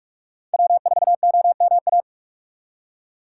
Morse code, heard as a keyed tone, about 42 wpm, 700 Hz. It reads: W4OMA
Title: USA Call Signs - 3x Character Spacing - 45wpm